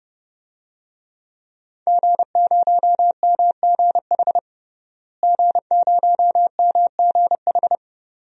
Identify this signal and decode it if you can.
Morse code, heard as a keyed tone, about 31 wpm, 700 Hz. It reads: G0MGH G0MGH
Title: DXCC-100 Call Signs - 5 Characters - Repeated 2x - 30wpm